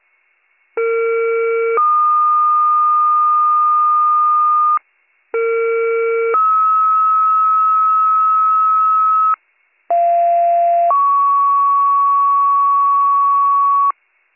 Two-tone page, heard as a one-second tone, then a three-second tone, about 460 then 1220 Hz, about 460 then 1290 Hz, about 690 then 1120 Hz. It is heard from 0.8 to 13.9 s.